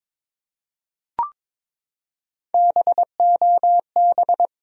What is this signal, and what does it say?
Morse code, heard as a keyed tone, about 21 wpm, 700 Hz. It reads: BOB
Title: USA Common Names - Max 4 Letters - 22wpm